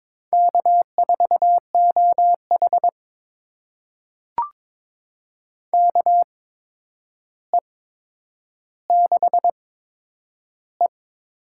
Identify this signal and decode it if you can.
Morse code, heard as a keyed tone, about 22 wpm, spaced as for 9 wpm, 700 Hz. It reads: K4OH KE6E
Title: USA Call Signs - 4 Characters - 8x Character Spacing - 22wpm